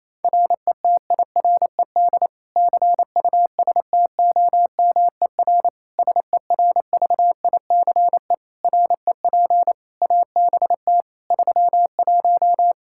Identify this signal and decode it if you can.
Morse code, heard as a keyed tone, about 28 wpm, 700 Hz. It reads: RETIRED CUSTOMER SERVICE REP ABT 31
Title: Practice QSO - Exchange 3 - More Info - 28wpm